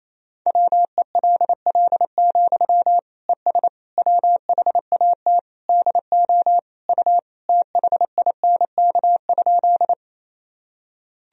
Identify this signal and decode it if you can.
Morse code, heard as a keyed tone, about 28 wpm, 700 Hz. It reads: WELL, ES WHAT DO U THINK?
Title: Sentences from Top 200 Words and Abbreviations 28wpm